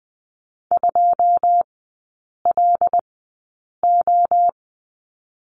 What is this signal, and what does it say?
Morse code, heard as a keyed tone, about 20 wpm, spaced as for 8 wpm, 690 Hz. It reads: 2LO
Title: 081 Lesson 14e - Number 2 - Three Characters 20wpm